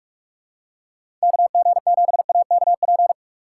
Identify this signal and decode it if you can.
Morse code, heard as a keyed tone, about 46 wpm, 700 Hz. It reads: KG7AKP
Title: USA Call Signs - 3x Character Spacing - 45wpm